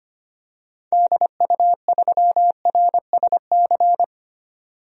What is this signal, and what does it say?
Morse code, heard as a keyed tone, about 25 wpm, 700 Hz. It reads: DU3RSC
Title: DXCC-100 Call Signs - 6 Characters - 25wpm